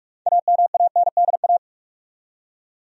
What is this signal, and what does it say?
Morse code, heard as a keyed tone, about 45 wpm, 700 Hz. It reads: AMANDA
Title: USA Common Names - 45wpm